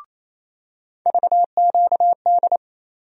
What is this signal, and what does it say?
Morse code, heard as a keyed tone, about 28 wpm, 700 Hz. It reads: VQD
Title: USA Call Sign Postfixes - 3 Letters 28wpm